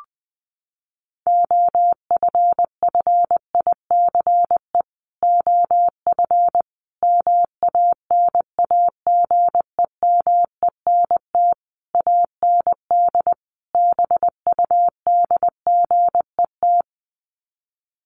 Morse code, heard as a keyed tone, about 20 wpm, 700 Hz. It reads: OFFICE OF MANAGEMENT AND BUDGET
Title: Sets of 5 Words Encore2 20wpm